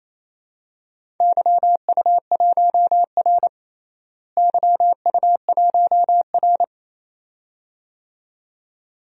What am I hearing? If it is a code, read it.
Morse code, heard as a keyed tone, about 28 wpm, 700 Hz. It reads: YU1R YU1R
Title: DXCC-100 Call Signs - 4 Characters - Repeated 2x - 28wpm